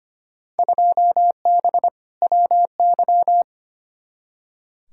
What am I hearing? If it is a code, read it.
Morse code, heard as a keyed tone, about 25 wpm, 700 Hz. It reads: 2B WY
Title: Field Day Exchanges Encore - 25wpm